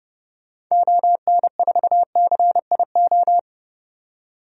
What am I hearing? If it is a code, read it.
Morse code, heard as a keyed tone, about 30 wpm, 700 Hz. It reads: ON4CIO
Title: DXCC-100 Call Signs - 6 Characters - 30wpm